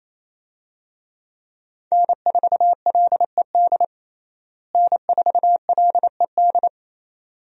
Morse code, heard as a keyed tone, about 28 wpm, 700 Hz. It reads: N4LED N4LED
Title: USA Call Signs - 5 Characters - Repeated 2x - 28wpm